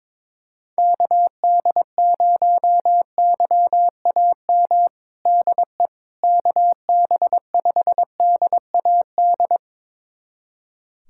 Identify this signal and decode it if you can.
Morse code, heard as a keyed tone, about 22 wpm, 700 Hz. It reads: KD0YAM DE KB5DAD
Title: USA Callsign DE Callsign - 6 Characters - 22wpm